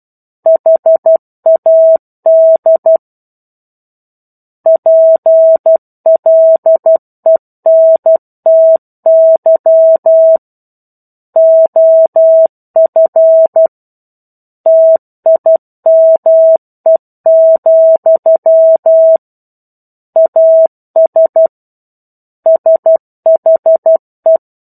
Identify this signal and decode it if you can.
Morse code, heard as a keyed tone, about 12 wpm, 650 Hz. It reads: HAD PLENTY OF TIME, AS SHE
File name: 01-Chapter-01 12 WPM